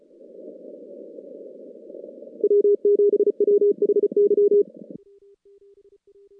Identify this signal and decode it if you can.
Morse code, heard as a keyed tone, about 32 wpm, 410 Hz. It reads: W7WHY
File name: contests / CW-SS-k5tr- -2200